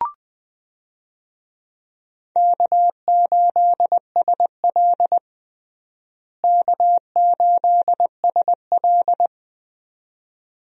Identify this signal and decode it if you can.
Morse code, heard as a keyed tone, about 21 wpm, 700 Hz. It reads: K8SL K8SL
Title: USA Call Signs - 4 Characters - Repeated 2x - 20wpm